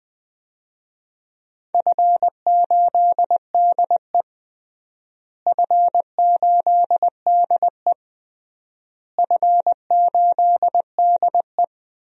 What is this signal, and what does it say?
Morse code, heard as a keyed tone, about 20 wpm, 700 Hz. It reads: F8DE F8DE F8DE
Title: DXCC-100 Call Signs - 4 Characters - Repeated 3x - 20wpm